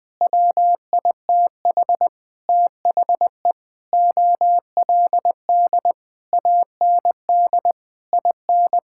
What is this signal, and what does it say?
Morse code, heard as a keyed tone, about 20 wpm, 700 Hz. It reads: WITH THE OLD AND IN
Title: Sentences with 9-Word Phrases - 20wpm